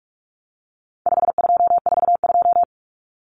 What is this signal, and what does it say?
Morse code, heard as a keyed tone, about 44 wpm, 700 Hz. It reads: 5242
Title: Numbers - 4-Digit 45wpm